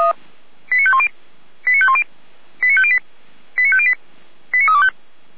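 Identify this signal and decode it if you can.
Five-tone selective call (ZVEI), heard as quick runs of five tones, tones of about 70 ms. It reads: 89510 89510 89508 89508 89325
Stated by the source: ZVEI five-tone selective call received with an SDR, from Feuerwehr Orlamünde.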